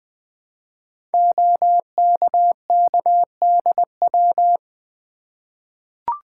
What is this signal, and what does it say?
Morse code, heard as a keyed tone, about 20 wpm, 700 Hz. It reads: OKKDW
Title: Code Groups - 5 Letter:Numbers - 20wpm